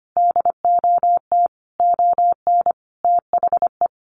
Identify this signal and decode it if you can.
Morse code, heard as a keyed tone, about 25 wpm, 700 Hz. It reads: DOT ON THE
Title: Sentences from Top 1500 Words 25wpm